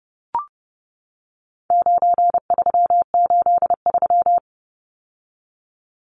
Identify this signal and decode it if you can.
Morse code, heard as a keyed tone, about 30 wpm, 700 Hz. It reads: 9383
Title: Numbers - 4-Digit 30wpm